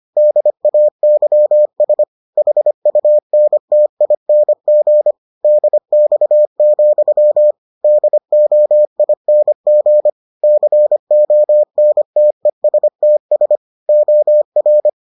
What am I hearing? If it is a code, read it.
Morse code, heard as a keyed tone, about 25 wpm, 600 Hz. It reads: DAYS HUNTING DX, DOING CONTESTS OR